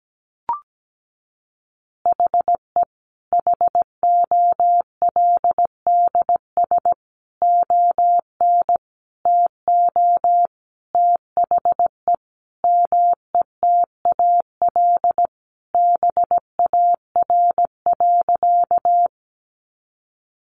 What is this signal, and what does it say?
Morse code, heard as a keyed tone, about 17 wpm, 700 Hz. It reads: HE HOLDS ON TO THE METAL BAR.
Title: Additional Easy Phrases 17wpm